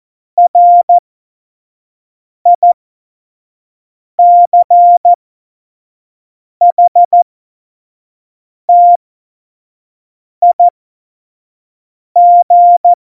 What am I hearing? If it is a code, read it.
Morse code, heard as a keyed tone, about 14 wpm, spaced as for 5 wpm, 700 Hz. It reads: RICHTIG
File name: Morsekurs / Witze / Witz 19 Mai 2020, 25 F70